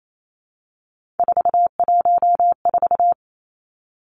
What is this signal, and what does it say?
Morse code, heard as a keyed tone, about 28 wpm, 700 Hz. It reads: 414
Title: Three Challenging Letter-Number Combinations 28wpm